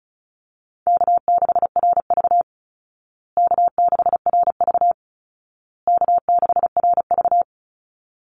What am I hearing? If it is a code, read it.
Morse code, heard as a keyed tone, about 34 wpm, 700 Hz. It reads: K6RV K6RV K6RV
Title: Call Signs Repeated 2x - 35wpm